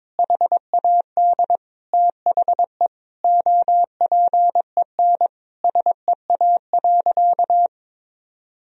Morse code, heard as a keyed tone, about 22 wpm, 700 Hz. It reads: HAD THE OPEN SEA.
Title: Sentences from Top 1000 Words - Review Part 1 - 22wpm